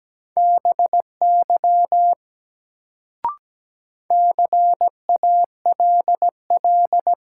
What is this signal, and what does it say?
Morse code, heard as a keyed tone, about 17 wpm, 700 Hz. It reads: BY CALL
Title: Sentences from Top 100 Words - New Word Drill - 17wpm